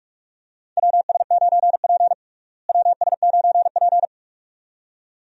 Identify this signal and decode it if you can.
Morse code, heard as a keyed tone, about 47 wpm, 700 Hz. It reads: WS9P WS9P
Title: USA Call Signs - 4 Characters - Repeated 2x - 45wpm